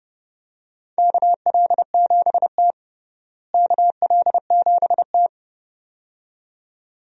Morse code, heard as a keyed tone, about 30 wpm, 700 Hz. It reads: KL7T KL7T
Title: DXCC-100 Call Signs - 4 Characters - Repeated 2x - 30wpm